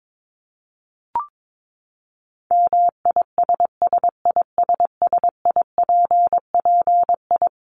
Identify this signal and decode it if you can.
Morse code, heard as a keyed tone, about 22 wpm, 700 Hz. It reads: MISSISSIPPI
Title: Dit Challenge 22wpm